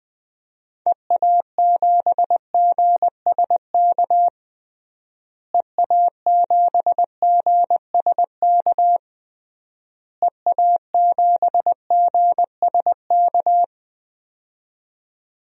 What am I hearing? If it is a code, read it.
Morse code, heard as a keyed tone, about 20 wpm, 700 Hz. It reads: EA7GSK EA7GSK EA7GSK
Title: DXCC-100 Call Signs - 6 Characters - Repeated 3x - 20wpm